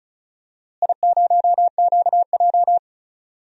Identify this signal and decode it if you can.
Morse code, heard as a keyed tone, about 34 wpm, 700 Hz. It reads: I0QJ
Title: DXCC-100 Call Signs - 4 Characters - Repeated 2x - 35wpm